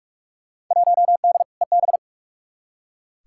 Morse code, heard as a keyed tone, about 41 wpm, 700 Hz. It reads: 1D EB
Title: Field Day Exchanges - 45wpm